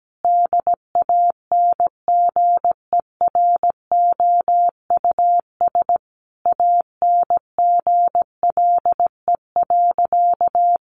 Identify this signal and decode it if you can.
Morse code, heard as a keyed tone, about 17 wpm, 700 Hz. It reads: DANGEROUS ANGLE.
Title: Sentences from Top 1500 Words 17wpm